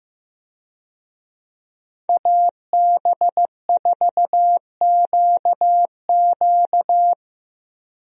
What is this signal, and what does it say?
Morse code, heard as a keyed tone, about 16 wpm, 700 Hz. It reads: AB4QQ
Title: USA Call Signs - 5 Characters - Repeated 2x - 15wpm